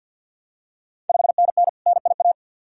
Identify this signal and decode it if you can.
Morse code, heard as a keyed tone, about 51 wpm, 690 Hz. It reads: 5NN NIA
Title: Ontario QSO Party Exchange 50wpm